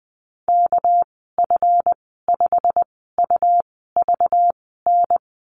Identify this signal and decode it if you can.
Morse code, heard as a keyed tone, about 20 wpm, spaced as for 15 wpm, 700 Hz. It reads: KF5UVN
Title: 245 Lesson 39g - Letter z - Call Signs 20wpm